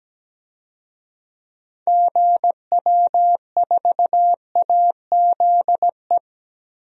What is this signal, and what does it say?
Morse code, heard as a keyed tone, about 17 wpm, 700 Hz. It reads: GW4AZE
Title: DXCC-100 Call Signs - 6 Characters - 17wpm